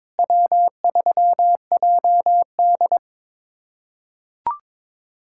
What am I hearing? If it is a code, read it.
Morse code, heard as a keyed tone, about 22 wpm, 700 Hz. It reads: W3JD
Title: USA Callsign DE Callsign - 4 Characters - 22wpm